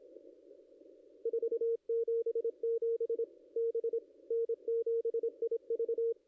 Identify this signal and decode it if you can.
Morse code, heard as a keyed tone, about 23 wpm, 440 Hz. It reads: 477 B N7IV
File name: contests / ss-cw-k5tr-op-n5rz -0600